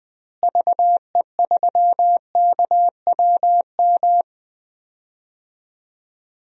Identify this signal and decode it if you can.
Morse code, heard as a keyed tone, about 20 wpm, 700 Hz. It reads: VE3KWM
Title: DXCC-100 Call Signs - 6 Characters - Repeated 3x - 20wpm